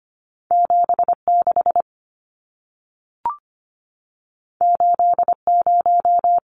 Morse code, heard as a keyed tone, about 25 wpm, 700 Hz. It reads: 76 80
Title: Numbers - 2-Digit 25wpm